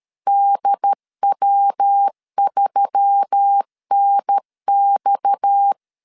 Morse code, heard as a keyed tone, about 13 wpm, 800 Hz. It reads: DW3NX
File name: CW / DET / 1 6 1500C S A 12wpm 3440